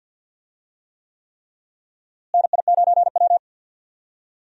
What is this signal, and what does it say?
Morse code, heard as a keyed tone, about 51 wpm, 700 Hz. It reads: NI9W
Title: USA Call Signs - 4 Characters - Repeated 3x - 50wpm